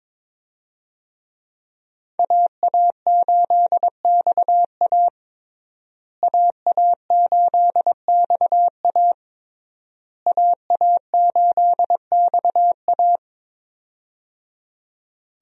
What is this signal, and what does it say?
Morse code, heard as a keyed tone, about 22 wpm, 700 Hz. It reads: AA8XA AA8XA AA8XA
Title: USA Call Signs - 5 Characters - Repeated 3x - 22wpm